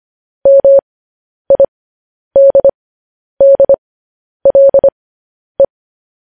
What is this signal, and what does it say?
Morse code, heard as a keyed tone, about 25 wpm, spaced as for 10 wpm, 550 Hz. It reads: MIDDLE